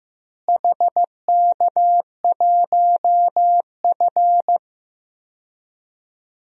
Morse code, heard as a keyed tone, about 15 wpm, 700 Hz. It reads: HK1F
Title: DXCC-100 Call Signs - 4 Characters - Repeated 3x - 15wpm